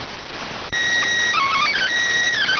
SSTV (slow-scan television), PD 290.